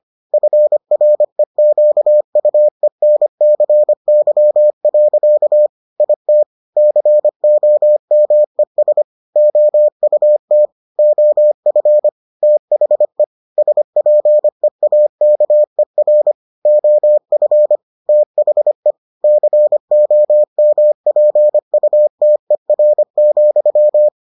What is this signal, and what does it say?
Morse code, heard as a keyed tone, about 25 wpm, 600 Hz. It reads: FREQUENCY. IT COMES OUT OF THE SPEAKER OF THE COMPUTER,